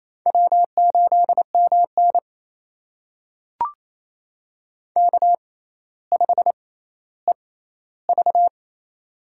Morse code, heard as a keyed tone, about 27 wpm, spaced as for 14 wpm, 700 Hz. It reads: W8MN K5EV
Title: USA Call Signs - 4 Characters - 6x Character Spacing - 28wpm